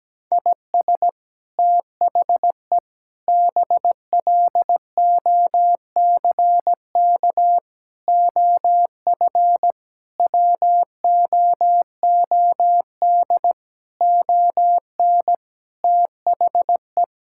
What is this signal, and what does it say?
Morse code, heard as a keyed tone, about 17 wpm, 700 Hz. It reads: IS THE BLOCK OF WOOD ON THE
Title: Sentences from Top 1600 Words 17wpm